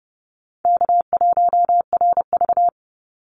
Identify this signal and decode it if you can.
Morse code, heard as a keyed tone, about 30 wpm, 700 Hz. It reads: K1RV
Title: Easy Call Signs - 30wpm